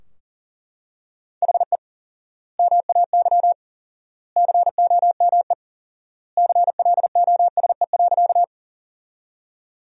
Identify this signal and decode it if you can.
Morse code, heard as a keyed tone, about 40 wpm, 700 Hz. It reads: HE MAY COME CLOSE.
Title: 4x Word-Spacing - Sentences from Top 300 Words - 40wpm